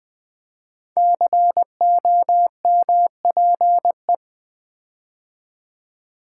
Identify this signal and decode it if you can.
Morse code, heard as a keyed tone, about 20 wpm, 700 Hz. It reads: COMPE
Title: Build A Word - 5 Letters At A Time - 12-Letter Words - 20wpm